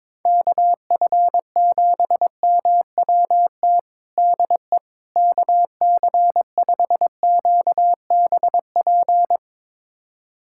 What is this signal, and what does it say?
Morse code, heard as a keyed tone, about 22 wpm, 700 Hz. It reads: KF7MWT DE KC5QBP
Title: USA Callsign DE Callsign - 6 Characters - 22wpm